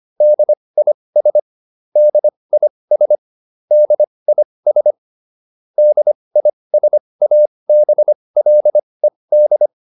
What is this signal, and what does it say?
Morse code, heard as a keyed tone, about 25 wpm, spaced as for 20 wpm, 600 Hz. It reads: DIS DIS DIS DISABLED